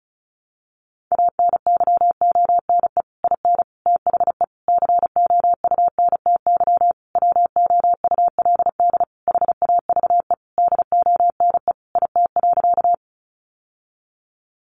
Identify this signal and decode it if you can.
Morse code, heard as a keyed tone, about 34 wpm, 700 Hz. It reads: ANYONE IN THE COUNTY WOULD HAVE DONE IT.
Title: Sentences from Top 1200 Words 35wpm